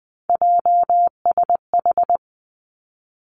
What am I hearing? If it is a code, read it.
Morse code, heard as a keyed tone, about 19 wpm, 700 Hz. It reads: JSH
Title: Three Challenging Letter-Number Combinations 20wpm